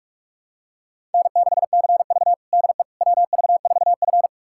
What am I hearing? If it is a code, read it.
Morse code, heard as a keyed tone, about 46 wpm, 700 Hz. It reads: N6CV DE WV4F
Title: USA Callsign DE Callsign - 4 Characters - 45wpm